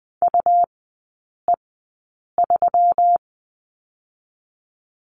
Morse code, heard as a keyed tone, about 20 wpm, 700 Hz. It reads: U E 3
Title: 121 Lesson 20e - Number 3 - Three Characters 20wpm